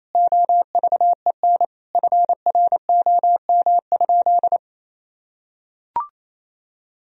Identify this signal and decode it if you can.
Morse code, heard as a keyed tone, about 28 wpm, 700 Hz. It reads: OVEN FROM?
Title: Sentences from Top 1000 Words - Review Part 4 - 28wpm